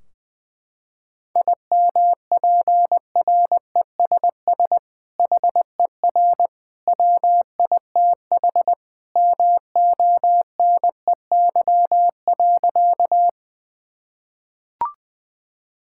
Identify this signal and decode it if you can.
Morse code, heard as a keyed tone, about 20 wpm, 700 Hz. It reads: IMPRESS HER WITH MONEY.
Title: Sentences from Top 1600 Words 20wpm